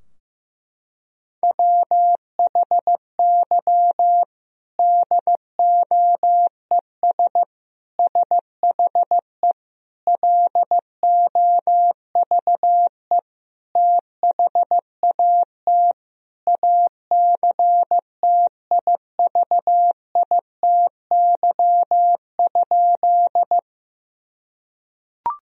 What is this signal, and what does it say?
Morse code, heard as a keyed tone, about 15 wpm, 700 Hz. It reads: WHY DOES SHE LOVE THAT ACTIVITY?
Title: Sentences from Top 1000 Words - Review Part 4 - 15wpm